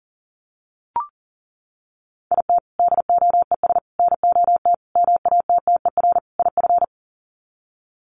Morse code, heard as a keyed tone, about 40 wpm, 700 Hz. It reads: IT DOES NOT MATTER IF
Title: Sets of 5 Words Encore 40wpm